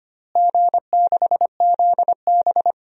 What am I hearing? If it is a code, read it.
Morse code, heard as a keyed tone, about 25 wpm, 700 Hz. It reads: G6ZB
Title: DXCC-100 Call Signs - 4 Characters - Repeated 3x - 25wpm